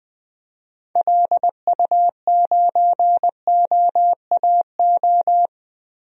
Morse code, heard as a keyed tone, about 20 wpm, 700 Hz. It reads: LU9OAO
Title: DXCC-100 Call Signs - 6 Characters - 20wpm